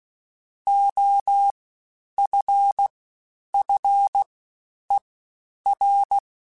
Morse code, heard as a keyed tone, about 16 wpm, 800 Hz. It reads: O F F E R
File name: Special INGs-mp3s / Special INGs-mp3s / SlowToFaster ing @15